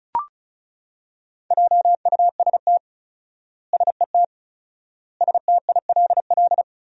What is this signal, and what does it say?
Morse code, heard as a keyed tone, about 34 wpm, 700 Hz. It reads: JUST SET STILL
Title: 4x Word-Spacing - Sentences from Top 300 Words - 35wpm